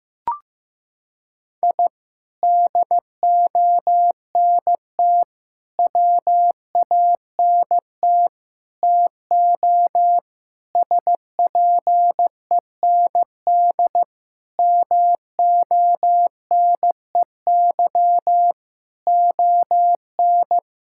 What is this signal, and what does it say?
Morse code, heard as a keyed tone, about 16 wpm, 700 Hz. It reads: I DONT WANT TO SPEND MONEY ON
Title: Sentences from Top 1000 Words - Review Part 5 - 15wpm